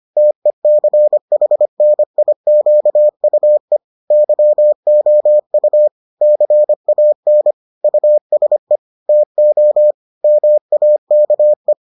Morse code, heard as a keyed tone, about 25 wpm, 600 Hz. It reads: TECHNIQUE YOU CAN USE TO MAKE